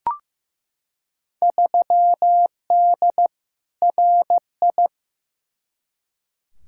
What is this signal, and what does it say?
Morse code, heard as a keyed tone, about 15 wpm, 700 Hz. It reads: 3D RI
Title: Field Day Exchanges Encore - 15wpm